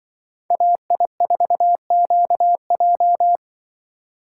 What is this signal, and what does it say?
Morse code, heard as a keyed tone, about 24 wpm, 700 Hz. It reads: AI4QJ
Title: Speed Racing - USA Call Signs 20wpm